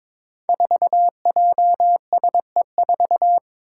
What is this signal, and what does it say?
Morse code, heard as a keyed tone, about 22 wpm, 700 Hz. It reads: 4JSE4
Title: Five Challenging Letter-Number Combinations 22wpm